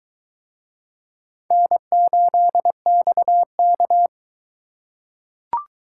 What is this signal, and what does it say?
Morse code, heard as a keyed tone, about 22 wpm, 700 Hz. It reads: N8XK